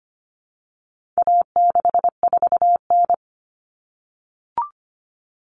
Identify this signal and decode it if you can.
Morse code, heard as a keyed tone, about 25 wpm, 700 Hz. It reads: A64N
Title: Common Cut Numbers - 4-Digit 25wpm